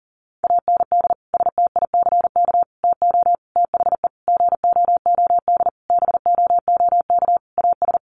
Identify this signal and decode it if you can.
Morse code, heard as a keyed tone, about 40 wpm, 700 Hz. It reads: AND STICK TO THE GOOD BOOK AS
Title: Sentences from Top 1200 Words 40wpm